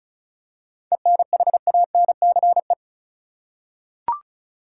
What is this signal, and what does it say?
Morse code, heard as a keyed tone, about 36 wpm, 700 Hz. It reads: ENHANCE
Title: Seven Letter Words - 35wpm